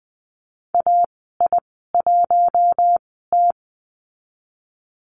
Morse code, heard as a keyed tone, about 20 wpm, spaced as for 14 wpm, 700 Hz. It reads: AI1T
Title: 042 Lesson 8g - Number 1 - Call Signs 20wpm